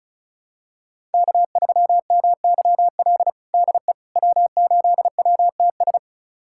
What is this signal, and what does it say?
Morse code, heard as a keyed tone, about 36 wpm, 690 Hz. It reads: K3MYL DE W8WTS
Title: USA Callsign DE Callsign - 5 Characters - 35wpm